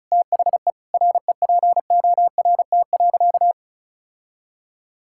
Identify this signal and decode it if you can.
Morse code, heard as a keyed tone, about 36 wpm, 700 Hz. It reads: THE REPORT.
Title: Sentences from Top 1000 Words - Review Part 6 - 35wpm